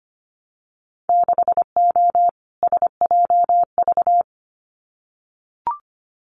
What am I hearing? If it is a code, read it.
Morse code, heard as a keyed tone, about 25 wpm, 690 Hz. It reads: 6O SJV
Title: Winter Field Day Exchanges Encore 25wpm